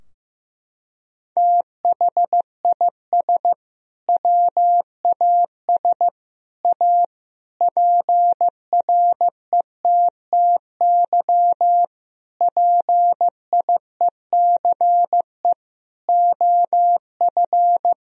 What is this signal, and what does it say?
Morse code, heard as a keyed tone, about 16 wpm, 700 Hz. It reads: THIS WAS A PRETTY PIECE OF
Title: Sentences from Top 1000 Words - Review Part 2 - 15wpm